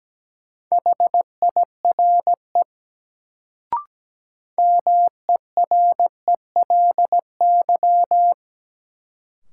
Morse code, heard as a keyed tone, about 17 wpm, 700 Hz. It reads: HIRE MERELY
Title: Sentences from Top 1400 Words - New Word Drill Alt - 17wpm